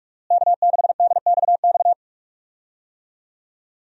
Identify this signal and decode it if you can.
Morse code, heard as a keyed tone, about 45 wpm, 700 Hz. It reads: K6DXX
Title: USA Call Signs - 5 Characters - Repeated 3x - 45wpm